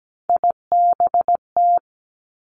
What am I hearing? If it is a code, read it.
Morse code, heard as a keyed tone, about 17 wpm, 700 Hz. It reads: IBT
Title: Three Challenging Letter-Number Combinations 17wpm